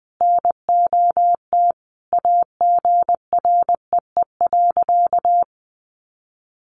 Morse code, heard as a keyed tone, about 19 wpm, 700 Hz. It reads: NOT AGREE.
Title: Sentences from Top 1100 Words 20wpm